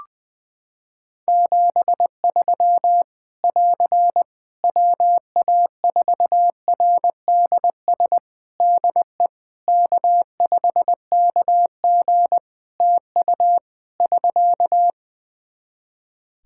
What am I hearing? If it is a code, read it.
Morse code, heard as a keyed tone, about 21 wpm, 700 Hz. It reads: 73 <AR> WA4RDS DE K5KG TU <SK>
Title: Practice QSO - Exchange 4 - Ending - 20wpm